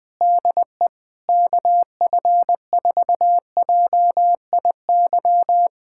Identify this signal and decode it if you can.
Morse code, heard as a keyed tone, about 21 wpm, 700 Hz. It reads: DE KF4JIY
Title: USA Callsign DE Callsign - 6 Characters - 20wpm